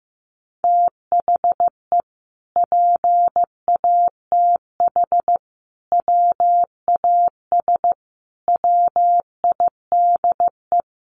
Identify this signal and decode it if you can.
Morse code, heard as a keyed tone, about 15 wpm, 700 Hz. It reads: THE PATH WAS WIDE
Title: Sentences from Top 1400 Words 15wpm